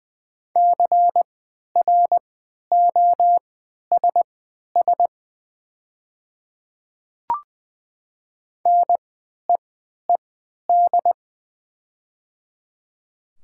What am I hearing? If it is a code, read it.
Morse code, heard as a keyed tone, about 20 wpm, spaced as for 11 wpm, 690 Hz. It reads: CROSS NEED
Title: 3x Character Spacing - Top 300 Words - 20wpm